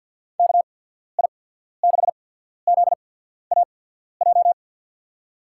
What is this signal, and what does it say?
Morse code, heard as a keyed tone, about 54 wpm, spaced as for 14 wpm, 700 Hz. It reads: KI6ZAJ
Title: USA Call Signs - 6 Characters - 8x Character Spacing - 50wpm